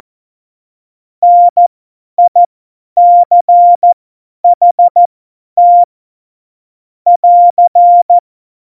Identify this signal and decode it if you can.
Morse code, heard as a keyed tone, about 14 wpm, spaced as for 10 wpm, 700 Hz. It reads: NICHT <AR>
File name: Morsekurs / Witze / Witz 19 Mai 2020, 50 F70